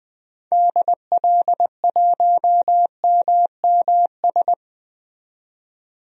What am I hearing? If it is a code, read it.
Morse code, heard as a keyed tone, about 20 wpm, 700 Hz. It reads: DL1MMS
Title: DXCC-100 Call Signs - 6 Characters - 20wpm